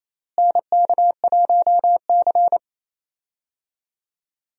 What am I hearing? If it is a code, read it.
Morse code, heard as a keyed tone, about 28 wpm, 700 Hz. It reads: NK1C
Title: USA Call Signs - 4 Characters - Repeated 2x - 28wpm